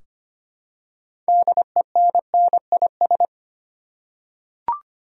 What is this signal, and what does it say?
Morse code, heard as a keyed tone, about 25 wpm, 700 Hz. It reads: DENNIS
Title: USA Common Names - 25wpm